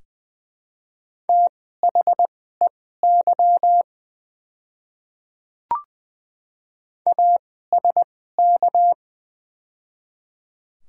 Morse code, heard as a keyed tone, about 20 wpm, spaced as for 15 wpm, 690 Hz. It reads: THEY ASK
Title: 2x Character Spacing - Top 200 Words - 20wpm